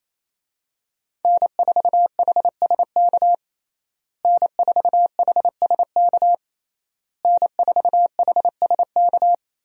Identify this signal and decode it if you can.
Morse code, heard as a keyed tone, about 28 wpm, 700 Hz. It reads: N4HSK N4HSK N4HSK
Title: USA Call Signs - 5 Characters - Repeated 3x - 28wpm